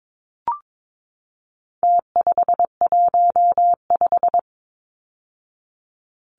Morse code, heard as a keyed tone, about 22 wpm, 700 Hz. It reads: T515
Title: Numbers - Contest Style - 4-Digit 22wpm